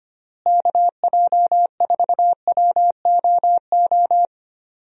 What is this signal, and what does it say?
Morse code, heard as a keyed tone, about 25 wpm, 700 Hz. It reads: KJ4WOO
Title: USA Call Signs - 6 Characters - Repeated 3x - 25wpm